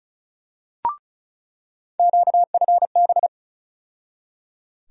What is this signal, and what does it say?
Morse code, heard as a keyed tone, about 36 wpm, 700 Hz. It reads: QFB
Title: USA Call Sign Postfixes - 3 Letters 35wpm